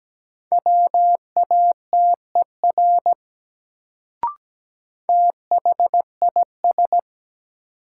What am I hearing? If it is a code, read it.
Morse code, heard as a keyed tone, about 17 wpm, 700 Hz. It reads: WATER THIS
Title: Sentences from Top 100 Words - New Word Drill - 17wpm